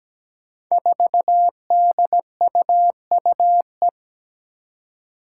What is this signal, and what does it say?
Morse code, heard as a keyed tone, about 17 wpm, 700 Hz. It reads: 4DUUE
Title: Five Challenging Letter-Number Combinations 17wpm